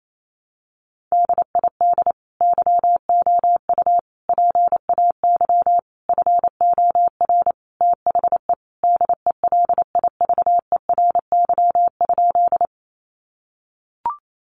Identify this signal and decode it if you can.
Morse code, heard as a keyed tone, about 28 wpm, 700 Hz. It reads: DID YOU PAY FOR THE DELIVERY?
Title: Sentences from Top 1500 Words 28wpm